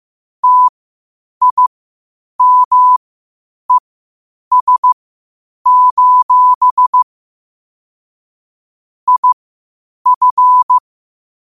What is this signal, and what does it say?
Morse code, heard as a keyed tone, about 15 wpm, spaced as for 8 wpm, 1000 Hz. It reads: TIMES: IF
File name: qotdfiles / 8wpm